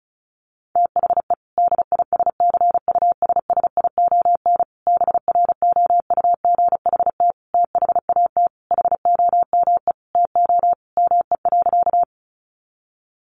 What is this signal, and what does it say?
Morse code, heard as a keyed tone, about 34 wpm, 700 Hz. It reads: THE DISCUSSION BROUGHT THAT HOME TO ME.
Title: Sentences from Top 1100 Words 35wpm